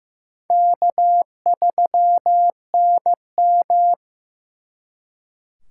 Morse code, heard as a keyed tone, about 18 wpm, 700 Hz. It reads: K3NM
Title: Speed Racing - USA 1x2 Call Signs 15wpm